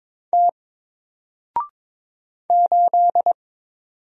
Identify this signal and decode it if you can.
Morse code, heard as a keyed tone, about 22 wpm, 700 Hz. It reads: T 8
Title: Single Character (Letter:Number:Punctuation:Prosign) 22wpm